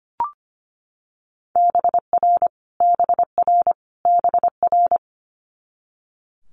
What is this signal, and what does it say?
Morse code, heard as a keyed tone, about 25 wpm, 700 Hz. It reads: BR BR BR
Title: Multi-letter Phonemes Repeated 3x 25wpm